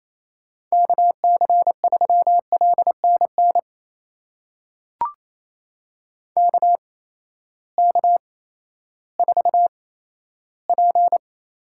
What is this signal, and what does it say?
Morse code, heard as a keyed tone, about 28 wpm, spaced as for 14 wpm, 700 Hz. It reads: KC3LNN KK4P
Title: USA Call Signs - 6 Characters - 8x Character Spacing - 28wpm